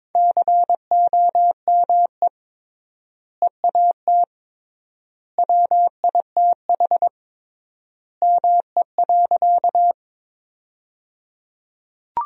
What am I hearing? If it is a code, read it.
Morse code, heard as a keyed tone, about 22 wpm, 700 Hz. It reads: COME EAT WITH ME.
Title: 3x Word-Spacing - Sentences from Top 300 Words - 22wpm